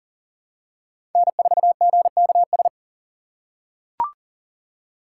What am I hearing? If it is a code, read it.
Morse code, heard as a keyed tone, about 39 wpm, 700 Hz. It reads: N4GKS
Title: USA Call Signs - 5 Characters - 2x Character Spacing - 40wpm